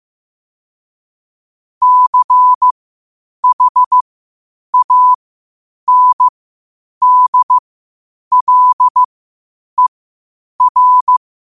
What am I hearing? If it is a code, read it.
Morse code, heard as a keyed tone, about 15 wpm, spaced as for 8 wpm, 1000 Hz. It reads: CHANDLER
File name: qotdfiles / 8wpm